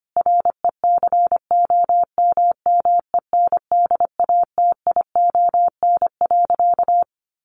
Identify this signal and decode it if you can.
Morse code, heard as a keyed tone, about 25 wpm, 700 Hz. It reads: RECOMMENDATION.
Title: Sentences from Top 1400 Words 25wpm